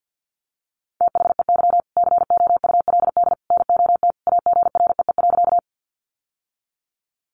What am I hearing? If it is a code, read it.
Morse code, heard as a keyed tone, about 49 wpm, 700 Hz. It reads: THEY COULD NOT AGREE.
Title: Sentences from Top 1100 Words 50wpm